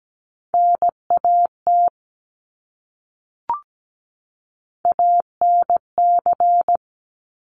Morse code, heard as a keyed tone, about 17 wpm, 700 Hz. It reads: NAT ANC
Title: Top Three Letter Combinations in Words 17wpm